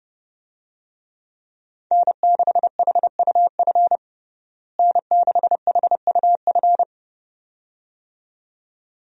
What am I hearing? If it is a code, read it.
Morse code, heard as a keyed tone, about 32 wpm, 700 Hz. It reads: N6HUF N6HUF
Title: USA Call Signs - 5 Characters - Repeated 2x - 30wpm